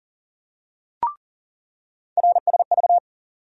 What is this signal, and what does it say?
Morse code, heard as a keyed tone, about 40 wpm, 700 Hz. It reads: RSV
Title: USA Call Sign Postfixes - 3 Letters 40wpm